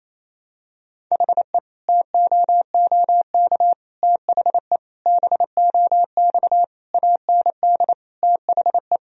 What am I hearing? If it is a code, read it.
Morse code, heard as a keyed tone, about 27 wpm, 700 Hz. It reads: HE TOOK THE BOX AND THE
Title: Sentences from Top 1000 Words - Review Part 2 - 28wpm